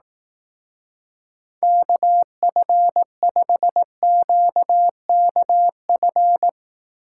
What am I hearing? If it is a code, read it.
Morse code, heard as a keyed tone, about 18 wpm, 700 Hz. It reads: KF5QKF
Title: Speed Racing - USA 2x3 Call Signs 15wpm